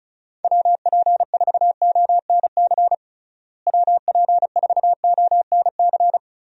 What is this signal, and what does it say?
Morse code, heard as a keyed tone, about 34 wpm, 700 Hz. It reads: WP4ONC WP4ONC
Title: USA Call Signs - 6 Characters - Repeated 3x - 35wpm